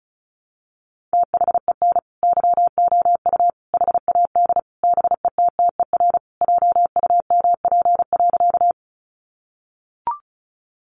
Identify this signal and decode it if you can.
Morse code, heard as a keyed tone, about 34 wpm, 700 Hz. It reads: THEN YOU HAD BETTER JUMP.
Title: Sentences from Top 1300 Words 35wpm